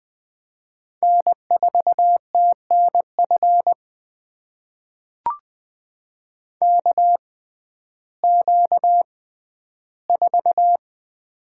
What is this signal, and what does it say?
Morse code, heard as a keyed tone, about 20 wpm, spaced as for 12 wpm, 700 Hz. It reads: N4TNF KQ4
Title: USA Call Signs - 5 Characters - 6x Character Spacing - 20wpm